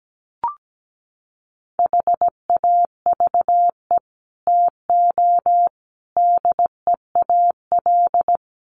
Morse code, heard as a keyed tone, about 17 wpm, 700 Hz. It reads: HAVE TO DEAL
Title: Sets of 5 Words Encore2 17wpm